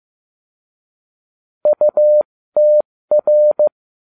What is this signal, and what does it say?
Morse code, heard as a keyed tone, about 15 wpm, spaced as for 13 wpm, 600 Hz. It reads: UTR